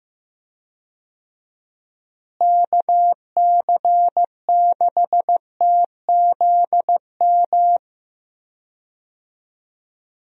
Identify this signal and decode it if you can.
Morse code, heard as a keyed tone, about 15 wpm, 700 Hz. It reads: KC6TZM
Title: USA Call Signs - 6 Characters - Repeated 2x - 15wpm